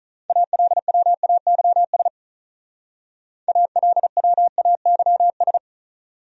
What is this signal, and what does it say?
Morse code, heard as a keyed tone, about 45 wpm, 700 Hz. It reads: ALWAYS ALWAYS